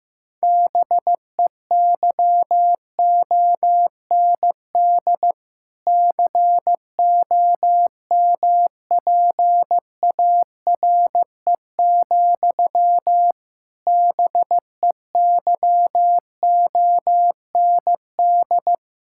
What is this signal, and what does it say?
Morse code, heard as a keyed tone, about 15 wpm, 700 Hz. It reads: BEYOND COMPARE, BEYOND
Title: Sentences from Top 1600 Words 15wpm